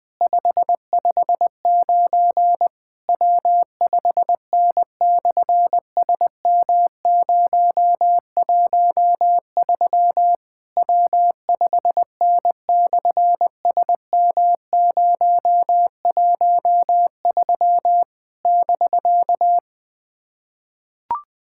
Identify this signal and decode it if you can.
Morse code, heard as a keyed tone, about 20 wpm, 700 Hz. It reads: 559 W5N/SM013 W5N/SM013 <BK>
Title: SOTA QSO Practice 20wpm